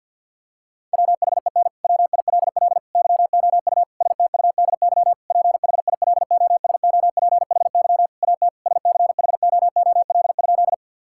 Morse code, heard as a keyed tone, about 50 wpm, 700 Hz. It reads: WHEN WILL YOU STUDY PHILOSOPHY AT SCHOOL?
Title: Sentences from Top 1000 Words - Review Part 6 - 50wpm